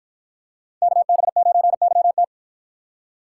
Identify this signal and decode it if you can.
Morse code, heard as a keyed tone, about 53 wpm, 700 Hz. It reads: KB9YT